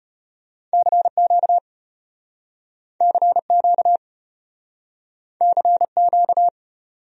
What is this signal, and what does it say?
Morse code, heard as a keyed tone, about 33 wpm, 700 Hz. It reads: CQ CQ CQ